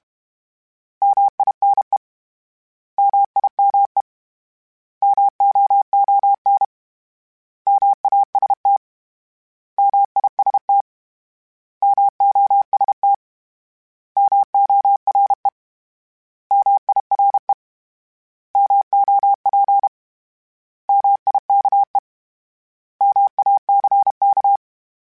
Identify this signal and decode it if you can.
Morse code, heard as a keyed tone, about 32 wpm, 800 Hz. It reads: MINE MIME MOON MAST MIST MOST MORE MIRE MOP MIKE MACK
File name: Misc / Misc-mp3s / Misc @30